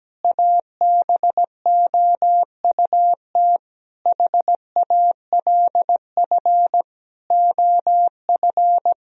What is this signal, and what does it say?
Morse code, heard as a keyed tone, about 17 wpm, 700 Hz. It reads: ABOUT HALF OF
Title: Sentences from Top 1000 Words - Review Part 1 - 17wpm